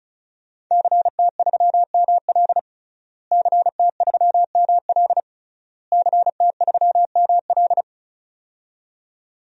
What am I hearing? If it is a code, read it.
Morse code, heard as a keyed tone, about 36 wpm, 700 Hz. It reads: CT3ML CT3ML CT3ML
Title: DXCC-100 Call Signs - 5 Characters - Repeated 3x - 35wpm